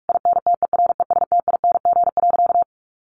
Morse code, heard as a keyed tone, about 44 wpm, 700 Hz. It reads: INTERESTING.
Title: Sentences from Top 1200 Words 45wpm